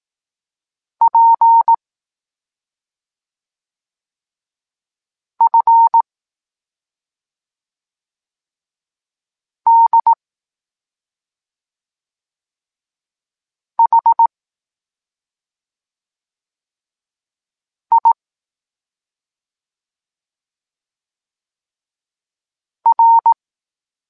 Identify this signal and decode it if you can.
Morse code, heard as a keyed tone, about 18 wpm, 920 Hz. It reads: P F D H I R
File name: Tekst-26 1